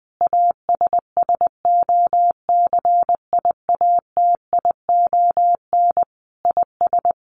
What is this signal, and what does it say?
Morse code, heard as a keyed tone, about 20 wpm, 700 Hz. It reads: ASSOCIATION IS
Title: Sentences from Top 1100 Words 20wpm